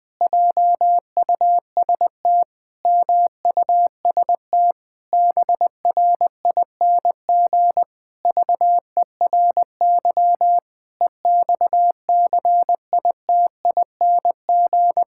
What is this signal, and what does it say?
Morse code, heard as a keyed tone, about 20 wpm, 700 Hz. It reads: JUST MUST BRING VERY EXCITING
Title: Pangram Challenge v2 - 20wpm